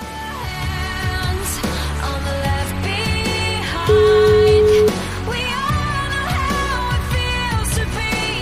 Ringback tone (Europe, 425 Hz), with noise.